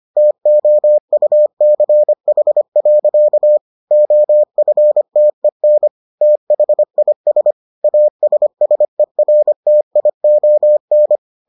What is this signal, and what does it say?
Morse code, heard as a keyed tone, about 25 wpm, 600 Hz. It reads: TOUCH. OFTEN THIS ASSERTION